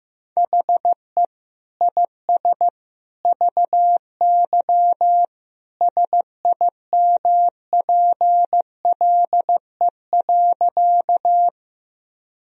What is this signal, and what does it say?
Morse code, heard as a keyed tone, about 15 wpm, 700 Hz. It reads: HE IS VY SIMPLE.
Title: Sentences from Top 400 Words and Abbreviations 15wpm